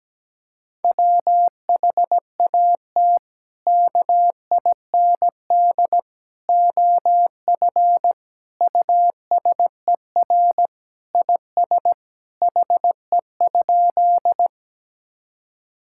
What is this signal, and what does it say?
Morse code, heard as a keyed tone, about 17 wpm, 700 Hz. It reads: WHAT KIND OF USER IS HE?
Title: Sentences from Top 1000 Words - Review Part 5 - 17wpm